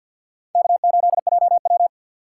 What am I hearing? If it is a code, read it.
Morse code, heard as a keyed tone, about 48 wpm, 700 Hz. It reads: K8JW
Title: USA Call Signs - 4x Character Spacing - 50wpm